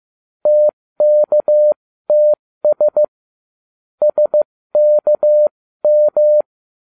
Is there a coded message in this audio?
Yes: Morse code from 0.5 to 6.4 s, heard as a keyed tone, about 15 wpm, spaced as for 13 wpm, 600 Hz. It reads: TKTS SKM